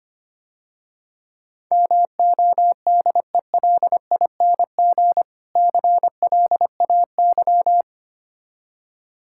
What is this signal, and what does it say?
Morse code, heard as a keyed tone, about 26 wpm, 700 Hz. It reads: MODELING CLAY
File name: practice / 25wpm / Difficult Words - Speed-Racing with Hints - 25wpm